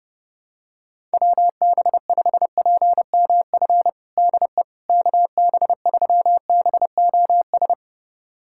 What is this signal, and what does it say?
Morse code, heard as a keyed tone, about 30 wpm, 700 Hz. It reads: WB5PMF DE KB3BOS
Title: USA Callsign DE Callsign 30wpm